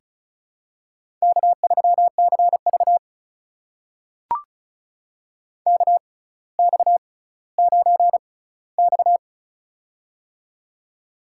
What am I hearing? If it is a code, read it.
Morse code, heard as a keyed tone, about 36 wpm, spaced as for 18 wpm, 700 Hz. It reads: K3CV KX9X
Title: USA Call Signs - 4 Characters - 6x Character Spacing - 35wpm